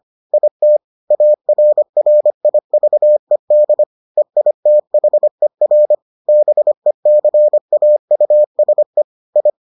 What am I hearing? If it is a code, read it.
Morse code, heard as a keyed tone, about 25 wpm, 600 Hz. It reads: IT ARRIVED EITHER BECAUSE I